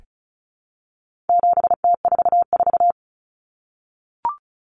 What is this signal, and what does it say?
Morse code, heard as a keyed tone, about 34 wpm, 700 Hz. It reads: 7T44
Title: Common Cut Numbers - 4-Digit 35wpm